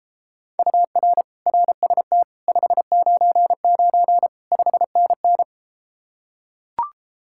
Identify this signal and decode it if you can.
Morse code, heard as a keyed tone, about 33 wpm, 700 Hz. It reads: UR RST 599 5NN